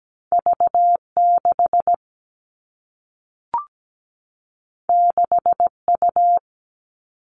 Morse code, heard as a keyed tone, about 17 wpm, 700 Hz. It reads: V6 6U
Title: Two Challenging Letter-Number Combinations 17wpm